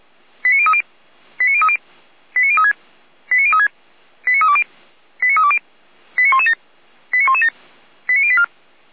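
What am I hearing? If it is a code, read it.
Five-tone selective call (ZVEI), heard as quick runs of five tones, tones of about 70 ms. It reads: 89030 89030 89036 89036 89320 89320 89117 89117 89084
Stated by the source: ZVEI five-tone call received by SDR, archived on the Feuerwehr Orlamünde server.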